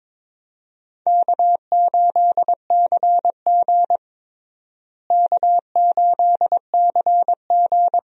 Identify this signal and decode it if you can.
Morse code, heard as a keyed tone, about 22 wpm, 700 Hz. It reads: K8CG K8CG
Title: USA Call Signs - 4 Characters - Repeated 3x - 22wpm